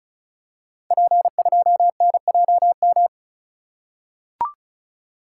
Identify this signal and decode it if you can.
Morse code, heard as a keyed tone, about 36 wpm, 700 Hz. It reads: P2NJM
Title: Code Groups - 5 Letter:Numbers - 35wpm